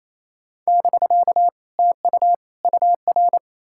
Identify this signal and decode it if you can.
Morse code, heard as a keyed tone, about 38 wpm, spaced as for 33 wpm, 700 Hz. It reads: <BK> TU UR
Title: Speed-Racing Common QSO Phrases - 28wpm